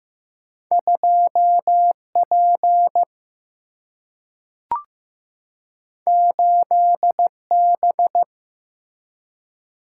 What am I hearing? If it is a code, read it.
Morse code, heard as a keyed tone, about 15 wpm, 700 Hz. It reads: 2 P 8 B
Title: Single-Letter Single-Number 15wpm